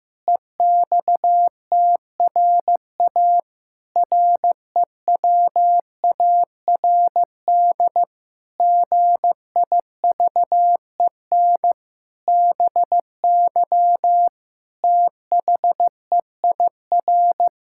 Morse code, heard as a keyed tone, about 15 wpm, 700 Hz. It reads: EXTRA REWARD GIVEN BY THEIR
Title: Pangram Challenge v2 - 15wpm